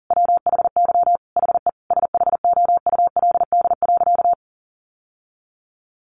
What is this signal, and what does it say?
Morse code, heard as a keyed tone, about 40 wpm, 700 Hz. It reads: WHY HE SHOULD.
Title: Sentences from Top 700 Words 40wpm